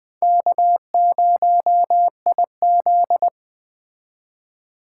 Morse code, heard as a keyed tone, about 20 wpm, 700 Hz. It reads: K0IZ
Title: Speed Racing - USA 1x2 Call Signs 17wpm